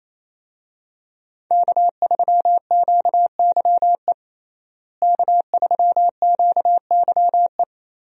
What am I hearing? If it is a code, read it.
Morse code, heard as a keyed tone, about 28 wpm, 700 Hz. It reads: K3QYE K3QYE
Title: USA Call Signs - 5 Characters - Repeated 3x - 28wpm